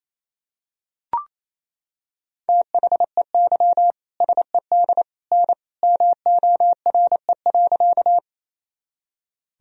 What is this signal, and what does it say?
Morse code, heard as a keyed tone, about 28 wpm, 700 Hz. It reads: THEY SED N MORE.
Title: Sentences from Top 100 Words and Abbreviations 28wpm